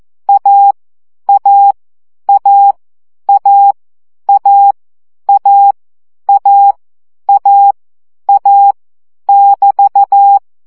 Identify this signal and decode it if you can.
Morse code, heard as a keyed tone, about 14 wpm, 800 Hz. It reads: A A A A A A A A A <BT>
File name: CW gyakorlas a 12 baud